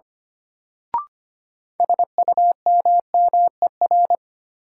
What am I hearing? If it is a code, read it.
Morse code, heard as a keyed tone, about 25 wpm, 700 Hz. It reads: SUMMER
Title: Sentences from Top 1000 Words - Review Part 6 - New Word Drill - 25wpm